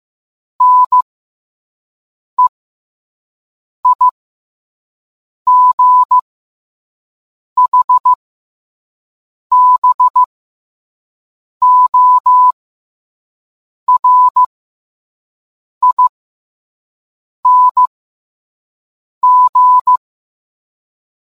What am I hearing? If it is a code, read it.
Morse code, heard as a keyed tone, about 15 wpm, spaced as for 5 wpm, 1000 Hz. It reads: NEIGHBORING